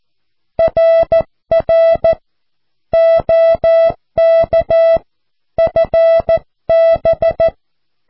Morse code, heard as a keyed tone, about 14 wpm, 660 Hz. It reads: RR OK FB
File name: cw / QSO